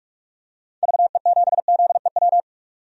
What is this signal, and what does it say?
Morse code, heard as a keyed tone, about 43 wpm, 700 Hz. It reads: VE7ZEW
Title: DXCC-100 Call Signs - 6 Characters - 45wpm